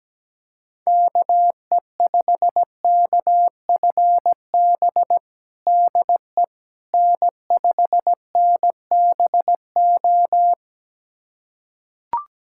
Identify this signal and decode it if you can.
Morse code, heard as a keyed tone, about 17 wpm, 700 Hz. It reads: KE5KFB DE N5NBO
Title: USA Callsign DE Callsign 17wpm